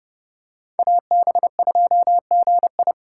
Morse code, heard as a keyed tone, about 30 wpm, 700 Hz. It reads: AB2GI
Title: K1USN SST QSO Practice - 30wpm